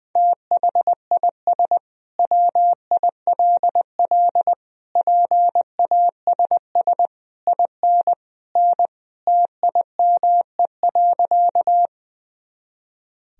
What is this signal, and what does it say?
Morse code, heard as a keyed tone, about 20 wpm, 700 Hz. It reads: THIS WILL PASS IN N TIME.
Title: Sentences from Top 400 Words and Abbreviations 20wpm